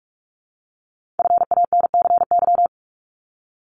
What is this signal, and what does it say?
Morse code, heard as a keyed tone, about 46 wpm, 690 Hz. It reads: FANCY
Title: Multi-letter Phonemes 45wpm